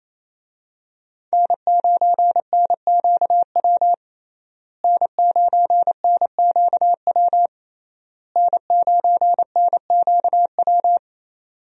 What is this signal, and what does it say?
Morse code, heard as a keyed tone, about 27 wpm, 700 Hz. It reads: N9NQW N9NQW N9NQW
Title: USA Call Signs - 5 Characters - Repeated 3x - 28wpm